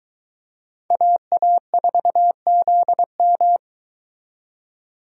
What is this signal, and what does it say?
Morse code, heard as a keyed tone, about 23 wpm, 700 Hz. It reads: AA4ZM